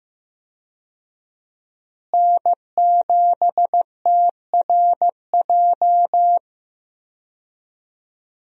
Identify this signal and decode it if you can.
Morse code, heard as a keyed tone, about 15 wpm, 700 Hz. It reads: N7TRJ
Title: USA Call Signs - 5 Characters - Repeated 3x - 15wpm